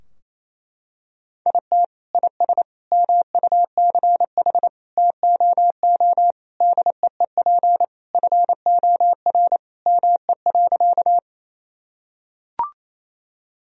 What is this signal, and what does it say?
Morse code, heard as a keyed tone, about 28 wpm, 700 Hz. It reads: IT IS MUCH TOO DEEP FOR ME.
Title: Sentences from Top 1000 Words - Review Part 3 - 28wpm